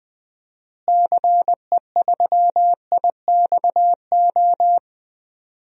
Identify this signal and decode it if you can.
Morse code, heard as a keyed tone, about 20 wpm, 700 Hz. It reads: CE3IXO
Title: DXCC-100 Call Signs - 6 Characters - Repeated 2x - 20wpm